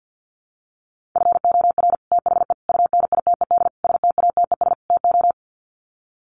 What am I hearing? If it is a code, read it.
Morse code, heard as a keyed tone, about 50 wpm, 700 Hz. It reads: FOR THE UNITED STATES TO
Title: Sets of 5 Words Encore 50wpm